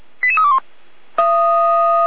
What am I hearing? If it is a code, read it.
Five-tone selective call (ZVEI), heard as quick runs of five tones, tones of about 70 ms. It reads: 88321